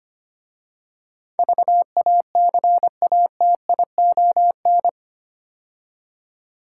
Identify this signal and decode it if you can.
Morse code, heard as a keyed tone, about 25 wpm, 700 Hz. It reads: VACATION
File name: 8 Letter Words - First 3 Letters Given - 25wpm